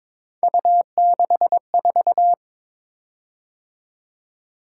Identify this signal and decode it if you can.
Morse code, heard as a keyed tone, about 22 wpm, 700 Hz. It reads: U64
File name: practice / 22wpm / Three Letter-Number Combinations - 22wpm